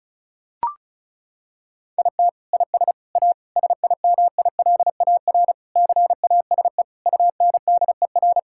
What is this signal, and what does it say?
Morse code, heard as a keyed tone, about 36 wpm, 700 Hz. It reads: IT IS A SIMILAR CASE UNDER
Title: Sentences from Top 1000 Words - Review Part 5 - 35wpm